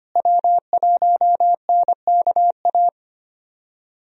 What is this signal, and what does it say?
Morse code, heard as a keyed tone, about 25 wpm, 700 Hz. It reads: W1NKA
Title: USA Call Signs - 5 Characters - Repeated 3x - 25wpm